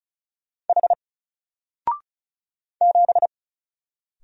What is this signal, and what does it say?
Morse code, heard as a keyed tone, about 36 wpm, 700 Hz. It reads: H 7
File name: Single-Letter-Number-35wpm